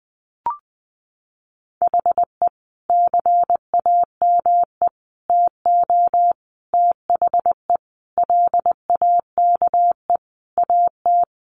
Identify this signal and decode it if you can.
Morse code, heard as a keyed tone, about 20 wpm, 700 Hz. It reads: HE CAME TO THE LAKE AT
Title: Sentences from Top 1100 Words 20wpm